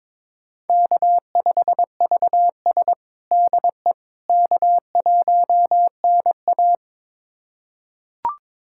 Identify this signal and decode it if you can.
Morse code, heard as a keyed tone, about 22 wpm, 700 Hz. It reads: K5VS DE K1NA
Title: USA Callsign DE Callsign - 4 Characters - 22wpm